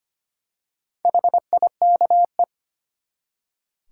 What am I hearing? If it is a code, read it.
Morse code, heard as a keyed tone, about 25 wpm, 700 Hz. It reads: HIKE
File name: practice / 25wpm / Easy Words with Hints - 25wpm